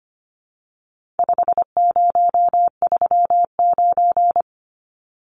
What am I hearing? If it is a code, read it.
Morse code, heard as a keyed tone, about 25 wpm, 700 Hz. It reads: 5039
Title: Numbers - 4-Digit 25wpm